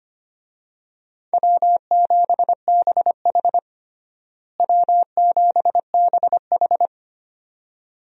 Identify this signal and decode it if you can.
Morse code, heard as a keyed tone, about 25 wpm, 700 Hz. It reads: W7BH W7BH
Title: USA Call Signs - 4 Characters - Repeated 2x - 25wpm